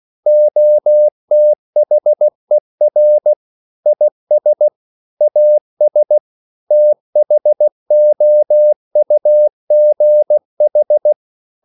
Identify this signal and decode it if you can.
Morse code, heard as a keyed tone, about 16 wpm, 600 Hz. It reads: OTHER IS AS THOUGH
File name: CW / Practice / EnglishQuotations0416